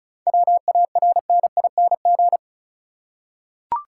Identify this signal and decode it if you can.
Morse code, heard as a keyed tone, about 34 wpm, 700 Hz. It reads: WARNING
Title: Seven Letter Words - 35wpm